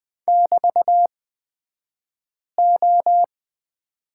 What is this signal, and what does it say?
Morse code, heard as a keyed tone, about 20 wpm, spaced as for 5 wpm, 700 Hz. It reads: <BT>O
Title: 254 Lesson 41d - Period - Two Characters 20wpm